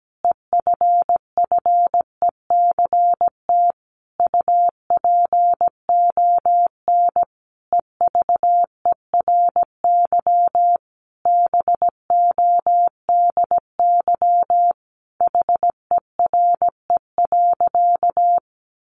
Morse code, heard as a keyed tone, about 17 wpm, 700 Hz. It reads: EFFECT UPON EVERY BODY HERE.
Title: Sentences from Top 1100 Words 17wpm